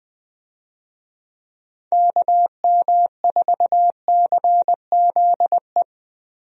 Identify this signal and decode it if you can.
Morse code, heard as a keyed tone, about 21 wpm, 700 Hz. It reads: KM4CZE
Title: USA Call Signs - 6 Characters - Repeated 3x - 20wpm